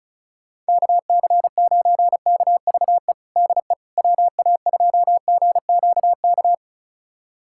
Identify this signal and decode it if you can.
Morse code, heard as a keyed tone, about 34 wpm, 700 Hz. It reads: KC9KVE DE WA2GQK
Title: USA Callsign DE Callsign - 6 Characters - 35wpm